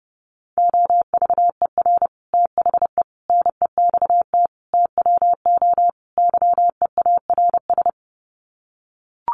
Morse code, heard as a keyed tone, about 30 wpm, 700 Hz. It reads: OVER THE NEXT TWO YEARS
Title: Sets of 5 Words Encore2 30wpm